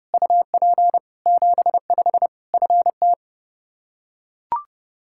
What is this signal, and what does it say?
Morse code, heard as a keyed tone, about 30 wpm, 700 Hz. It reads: UP 75 FT
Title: Practice QSO - Exchange 2 - Info - 30wpm